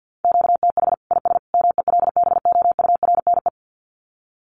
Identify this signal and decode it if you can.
Morse code, heard as a keyed tone, about 50 wpm, 700 Hz. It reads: QTH IS MELBOURNE
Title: Common QSO Phrases 50wpm